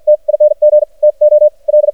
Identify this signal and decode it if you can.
Morse code, heard as a keyed tone, about 58 wpm, spaced as for 47 wpm, 600 Hz. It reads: TING TO L
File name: archived / CRSnet